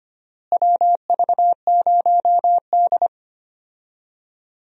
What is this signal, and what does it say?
Morse code, heard as a keyed tone, about 25 wpm, 700 Hz. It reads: WV0D
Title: USA Call Signs - 4 Characters - Repeated 3x - 25wpm